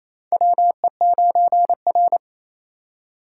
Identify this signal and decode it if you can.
Morse code, heard as a keyed tone, about 28 wpm, 700 Hz. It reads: WE9R
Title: USA Call Signs - 4 Characters - 3x Character Spacing - 28wpm